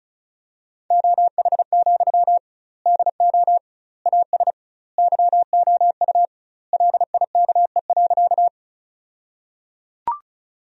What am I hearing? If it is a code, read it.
Morse code, heard as a keyed tone, about 34 wpm, 700 Hz. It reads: OH, DO AS YOU LIKE.
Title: 2x Word-Spacing - Sentences from Top 500 Words - 35wpm